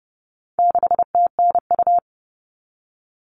Morse code, heard as a keyed tone, about 30 wpm, 700 Hz. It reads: 6TNU
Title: All Cut Numbers - 4-Digit 30wpm